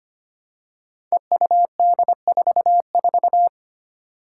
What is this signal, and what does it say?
Morse code, heard as a keyed tone, about 25 wpm, 700 Hz. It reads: EUD44
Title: Five Challenging Letter-Number Combinations 25wpm